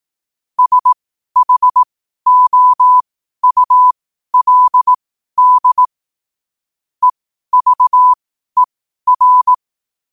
Morse code, heard as a keyed tone, about 18 wpm, spaced as for 12 wpm, 1000 Hz. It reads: SHOULD EVER
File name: qotdfiles / 12wpm